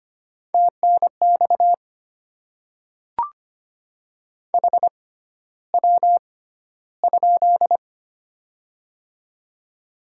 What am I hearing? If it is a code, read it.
Morse code, heard as a keyed tone, about 25 wpm, spaced as for 13 wpm, 700 Hz. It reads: TNX HW?
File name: Top 20 QSO Elements - 6x Character Spacing - 25wpm